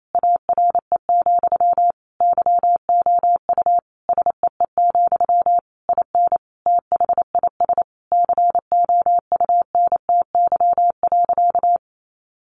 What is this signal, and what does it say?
Morse code, heard as a keyed tone, about 28 wpm, 700 Hz. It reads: ARE, YOU SEE, IN THIS COUNTY.
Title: Sentences from Top 1200 Words 28wpm